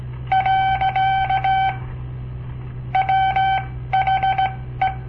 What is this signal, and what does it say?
Morse code, heard as a keyed tone, about 17 wpm, spaced as for 13 wpm, 770 Hz. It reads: . WHE
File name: papertapes / local / Code Practice-01